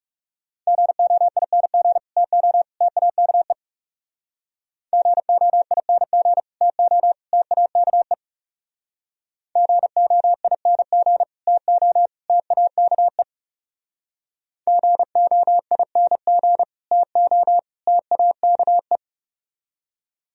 Morse code, heard as a keyed tone, about 40 wpm, 700 Hz. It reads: GOING TO TAKE GOING TO TAKE GOING TO TAKE GOING TO TAKE